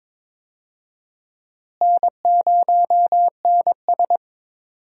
Morse code, heard as a keyed tone, about 22 wpm, 700 Hz. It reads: N0NS
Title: USA Call Signs - 4 Characters - Repeated 2x - 22wpm